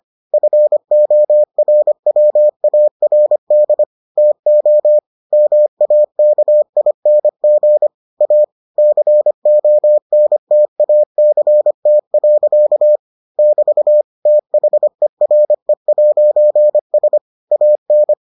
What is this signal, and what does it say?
Morse code, heard as a keyed tone, about 25 wpm, 600 Hz. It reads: FORWARD TO MAKING A CONTACT. <BT> THERE'S AN